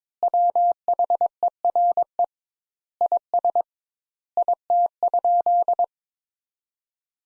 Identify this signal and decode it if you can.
Morse code, heard as a keyed tone, about 22 wpm, 700 Hz. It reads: WHERE IS IT?
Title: Sentences Top 200 Words - 22wpm